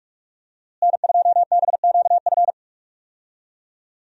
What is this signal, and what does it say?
Morse code, heard as a keyed tone, about 46 wpm, 700 Hz. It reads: N2BQF
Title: USA Call Signs - 5 Characters - Repeated 3x - 45wpm